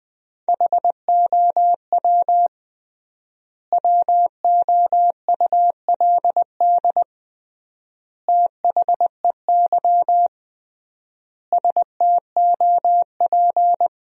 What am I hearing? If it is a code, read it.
Morse code, heard as a keyed tone, about 21 wpm, 700 Hz. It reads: HOW WOULD THEY STOP
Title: 3x Word-Spacing - Sentences from Top 300 Words - 20wpm